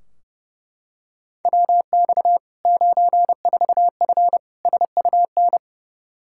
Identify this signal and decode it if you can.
Morse code, heard as a keyed tone, about 30 wpm, 700 Hz. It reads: WX 94F SUN
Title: Practice QSO - Exchange 2 - Info - 30wpm